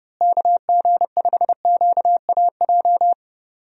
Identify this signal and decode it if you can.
Morse code, heard as a keyed tone, about 30 wpm, 700 Hz. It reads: KG5QAJ
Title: USA Call Signs - 6 Characters - Repeated 2x - 30wpm